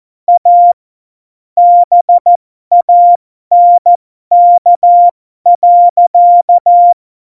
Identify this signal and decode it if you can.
Morse code, heard as a keyed tone, about 14 wpm, spaced as for 12 wpm, 700 Hz. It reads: A BANK.
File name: Morsekurs / Witze